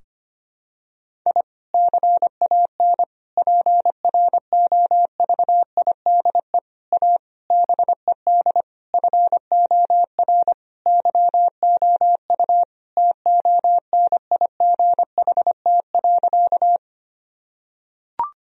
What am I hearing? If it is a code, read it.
Morse code, heard as a keyed tone, about 25 wpm, 700 Hz. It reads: I CAN PROVIDE A BED FOR YOU TONIGHT.
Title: Sentences from Top 1000 Words - Review Part 3 - 25wpm